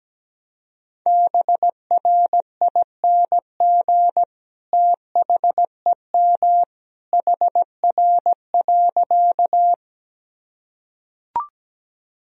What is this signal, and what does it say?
Morse code, heard as a keyed tone, about 17 wpm, 700 Hz. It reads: BRING THEM HR.
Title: Sentences from Top 500 Words and Abbreviations 17wpm